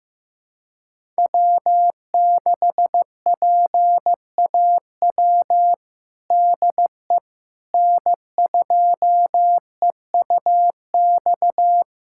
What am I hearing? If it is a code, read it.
Morse code, heard as a keyed tone, about 15 wpm, 700 Hz. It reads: W6PAW DE N2EUX
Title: USA Callsign DE Callsign - 5 Characters - 15wpm